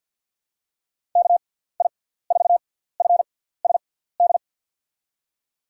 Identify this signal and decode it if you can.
Morse code, heard as a keyed tone, about 53 wpm, spaced as for 17 wpm, 700 Hz. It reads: KI4FSD
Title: USA Call Signs - 6x Character Spacing - 50wpm